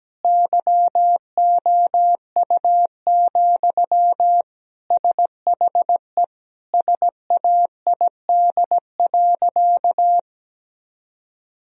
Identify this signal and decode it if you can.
Morse code, heard as a keyed tone, about 17 wpm, 700 Hz. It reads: YOU, SHE SAID.
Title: Sentences from Top 1000 Words - Review Part 1 - 17wpm